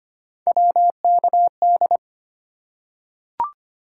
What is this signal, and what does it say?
Morse code, heard as a keyed tone, about 25 wpm, 700 Hz. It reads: WKD
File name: Three Letter-Number Combinations - 25wpm